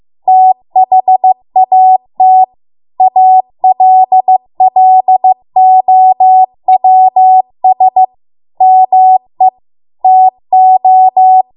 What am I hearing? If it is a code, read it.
Morse code, heard as a keyed tone, about 15 wpm, 750 Hz. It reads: THAT ALLOWS ME TO